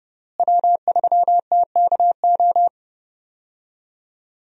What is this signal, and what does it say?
Morse code, heard as a keyed tone, about 31 wpm, 700 Hz. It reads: W3TKO
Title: USA Call Signs - 5 Characters - Repeated 3x - 30wpm